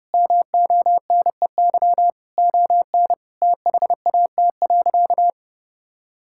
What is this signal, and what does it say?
Morse code, heard as a keyed tone, about 31 wpm, 700 Hz. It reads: MONEY ON THAT.
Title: Sentences from Top 1000 Words - Review Part 5 - 30wpm